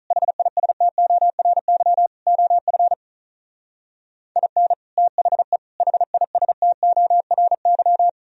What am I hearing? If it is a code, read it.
Morse code, heard as a keyed tone, about 45 wpm, 700 Hz. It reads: HISTORY OF IN THE HISTORY